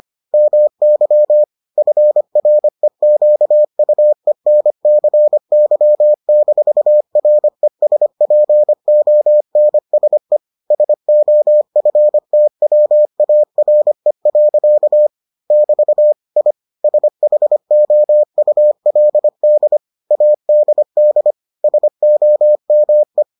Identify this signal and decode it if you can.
Morse code, heard as a keyed tone, about 25 wpm, 600 Hz. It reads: MY FREQUENCY-RESPONSE SOFTWARE. <BT> I SHOULD ADD SOME